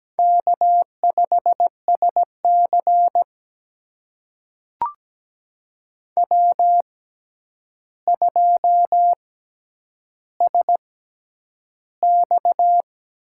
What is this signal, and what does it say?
Morse code, heard as a keyed tone, about 17 wpm, spaced as for 9 wpm, 700 Hz. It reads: K5SC W2SX
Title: USA Call Signs - 4 Characters - 6x Character Spacing - 17wpm